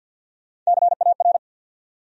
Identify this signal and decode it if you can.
Morse code, heard as a keyed tone, about 68 wpm, spaced as for 59 wpm, 700 Hz. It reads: CAR
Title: Speed-Racing Top 500 Words 50wpm